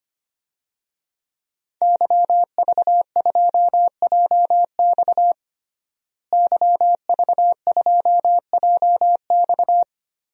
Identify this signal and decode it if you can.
Morse code, heard as a keyed tone, about 25 wpm, 700 Hz. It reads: YV2JX YV2JX
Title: DXCC-100 Call Signs - 5 Characters - Repeated 3x - 25wpm